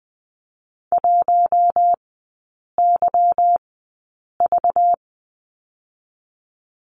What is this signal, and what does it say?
Morse code, heard as a keyed tone, about 20 wpm, spaced as for 8 wpm, 700 Hz. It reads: 1YV
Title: 212 Lesson 34e - Number 8 - Three Characters 20wpm